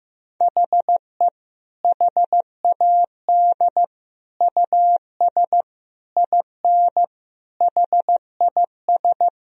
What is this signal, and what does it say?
Morse code, heard as a keyed tone, about 16 wpm, 700 Hz. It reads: HE HAD US IN HIS
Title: Sentences from Top 1000 Words - Review Part 2 - 15wpm